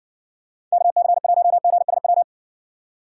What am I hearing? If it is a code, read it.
Morse code, heard as a keyed tone, about 64 wpm, 700 Hz. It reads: KC1GSW